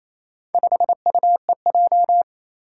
Morse code, heard as a keyed tone, about 28 wpm, 700 Hz. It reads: 5UEJ
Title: Four Challenging Letter-Number Combinations 28wpm